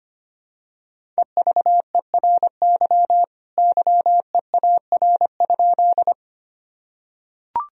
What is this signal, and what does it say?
Morse code, heard as a keyed tone, about 25 wpm, 700 Hz. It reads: EVERY YEAR?
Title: Sentences from Top 1000 Words - Review Part 1 - 25wpm